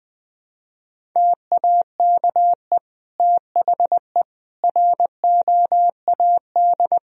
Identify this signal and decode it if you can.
Morse code, heard as a keyed tone, about 20 wpm, 700 Hz. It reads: TAKE THE ROAD
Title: Sentences from Top 400 Words and Abbreviations 20wpm